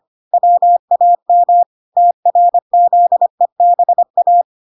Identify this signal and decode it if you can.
Morse code, heard as a keyed tone, about 25 wpm, 700 Hz. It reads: WAM TRZEBA